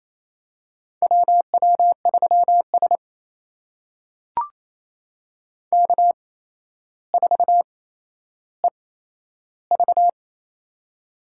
Morse code, heard as a keyed tone, about 28 wpm, spaced as for 12 wpm, 700 Hz. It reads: WW3S K4EV
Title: USA Call Signs - 4 Characters - 8x Character Spacing - 28wpm